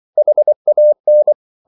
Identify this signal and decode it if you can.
Morse code, heard as a keyed tone, about 24 wpm, 600 Hz. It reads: HAN
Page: Ferie 24 wpm